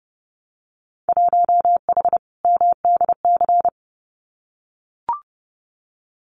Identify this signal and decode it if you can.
Morse code, heard as a keyed tone, about 30 wpm, 700 Hz. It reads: 1H MDC
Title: Winter Field Day Exchanges 30wpm